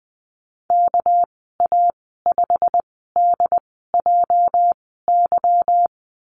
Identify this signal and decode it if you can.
Morse code, heard as a keyed tone, about 20 wpm, spaced as for 15 wpm, 700 Hz. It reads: KA5DJY
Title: 219 Lesson 35g - Number 0 - Call Signs 20wpm